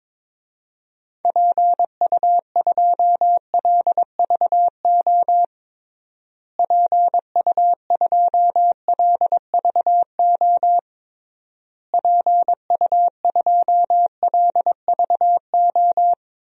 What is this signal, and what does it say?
Morse code, heard as a keyed tone, about 22 wpm, 700 Hz. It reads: PU2LVO PU2LVO PU2LVO
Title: DXCC-100 Call Signs - 6 Characters - Repeated 3x - 22wpm